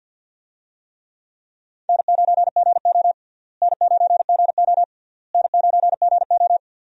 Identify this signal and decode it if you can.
Morse code, heard as a keyed tone, about 46 wpm, 700 Hz. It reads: N9GO N9GO N9GO
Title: USA Call Signs - 4 Characters - Repeated 3x - 50wpm